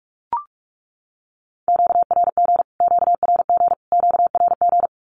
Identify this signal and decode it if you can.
Morse code, heard as a keyed tone, about 44 wpm, 700 Hz. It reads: QRG QRG QRG
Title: Q-Codes Repeated 3x 45wpm